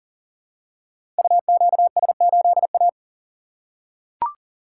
Morse code, heard as a keyed tone, about 42 wpm, 700 Hz. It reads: UQS8A
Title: Code Groups - 5 Letter:Numbers - 40wpm